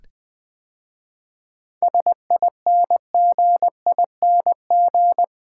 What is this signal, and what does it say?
Morse code, heard as a keyed tone, about 20 wpm, 690 Hz. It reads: SINGING
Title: 7 Letter Words - First 3 Letters Given - 20wpm